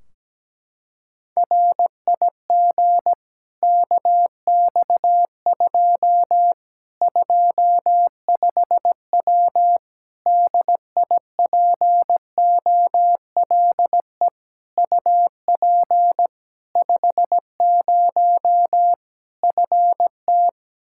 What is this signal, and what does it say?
Morse code, heard as a keyed tone, about 17 wpm, 700 Hz. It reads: RIG KX2 25W DIPOLE UP 50 FT
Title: Practice QSO - Exchange 2 - Info - 17wpm